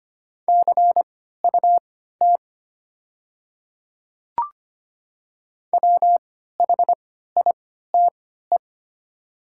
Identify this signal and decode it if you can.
Morse code, heard as a keyed tone, about 25 wpm, spaced as for 14 wpm, 700 Hz. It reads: CUT WHITE
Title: 3x Character Spacing - Top 300 Words - 25wpm